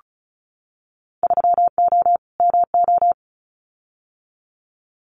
Morse code, heard as a keyed tone, about 34 wpm, 700 Hz. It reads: 3O MO
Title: Winter Field Day Exchanges Encore 35wpm